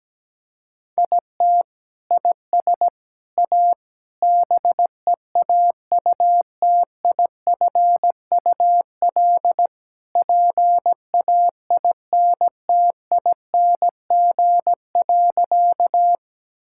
Morse code, heard as a keyed tone, about 17 wpm, 700 Hz. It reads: IT IS A BEAUTIFUL PAINTING.
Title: Sentences from Top 1000 Words - Review Part 6 - 17wpm